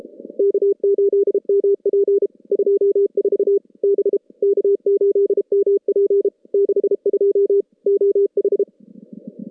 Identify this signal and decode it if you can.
Morse code, heard as a keyed tone, about 32 wpm, 400 Hz. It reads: K8MP 24 B K8MP 62 OH
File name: contests / CW-SS-k5tr- -2100